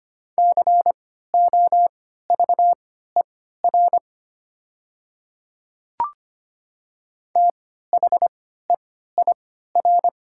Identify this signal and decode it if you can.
Morse code, heard as a keyed tone, about 25 wpm, spaced as for 14 wpm, 700 Hz. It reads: COVER THEIR
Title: 3x Character Spacing - Top 300 Words - 25wpm